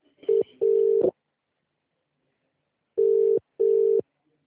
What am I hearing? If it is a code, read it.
Ringback tone (UK), 400 + 450 Hz.